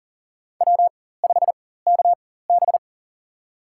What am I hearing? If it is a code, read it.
Morse code, heard as a keyed tone, about 40 wpm, spaced as for 19 wpm, 700 Hz. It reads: W5KB
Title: USA Call Signs - 4x Character Spacing - 40wpm